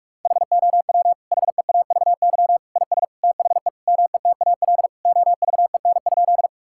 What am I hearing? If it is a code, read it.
Morse code, heard as a keyed tone, about 41 wpm, 700 Hz. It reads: HOW HEAVY IS THE METAL OVEN?
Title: Sentences from Top 1000 Words - Review Part 6 - 45wpm